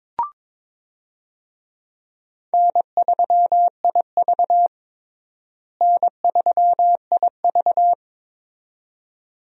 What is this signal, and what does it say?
Morse code, heard as a keyed tone, about 22 wpm, 700 Hz. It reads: N3IV N3IV
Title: USA Call Signs - 4 Characters - Repeated 2x - 22wpm